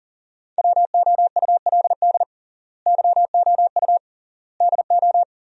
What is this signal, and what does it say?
Morse code, heard as a keyed tone, about 40 wpm, 700 Hz. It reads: WOULD YOU DO
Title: 3x Word-Spacing - Sentences from Top 100 Words - 40wpm